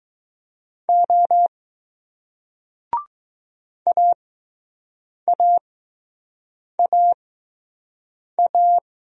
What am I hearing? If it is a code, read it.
Morse code, heard as a keyed tone, about 20 wpm, spaced as for 6 wpm, 700 Hz. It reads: O AAAA